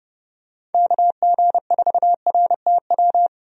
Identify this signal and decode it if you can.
Morse code, heard as a keyed tone, about 30 wpm, 700 Hz. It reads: KG4RTW
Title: USA Call Signs - 6 Characters - 30wpm